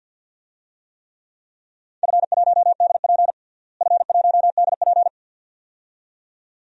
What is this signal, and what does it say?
Morse code, heard as a keyed tone, about 49 wpm, 700 Hz. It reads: F1DP F1DP
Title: DXCC-100 Call Signs - 4 Characters - Repeated 2x - 50wpm